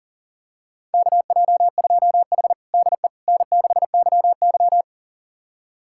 Morse code, heard as a keyed tone, about 37 wpm, 700 Hz. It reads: KJ2H DE N6YY
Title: USA Callsign DE Callsign - 4 Characters - 40wpm